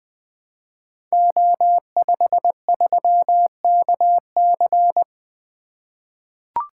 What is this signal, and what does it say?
Morse code, heard as a keyed tone, about 20 wpm, 700 Hz. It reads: O53KC
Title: Code Groups - 5 Letter:Numbers - 20wpm